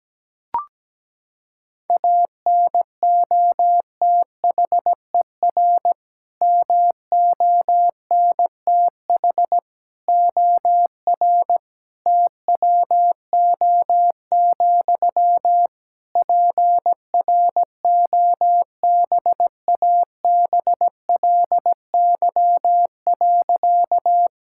Morse code, heard as a keyed tone, about 17 wpm, 700 Hz. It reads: ANOTHER MONTH OR TWO, PROBABLY.
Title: Sentences from Top 1000 Words - Review Part 5 - 17wpm